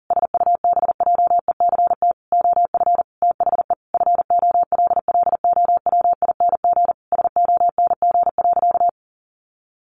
Morse code, heard as a keyed tone, about 40 wpm, 700 Hz. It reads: SUBJECT OF THE FOLLOWING SONG.
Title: Sentences from Top 900 Words 40wpm